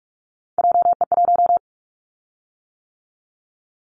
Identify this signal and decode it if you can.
Morse code, heard as a keyed tone, about 46 wpm, 700 Hz. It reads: JE1
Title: Callsign Prefixes for Top 25 DXCC - 45wpm